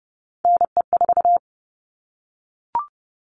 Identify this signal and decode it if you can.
Morse code, heard as a keyed tone, about 30 wpm, 700 Hz. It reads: NE4
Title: All Cut Numbers - 3-Digit 30wpm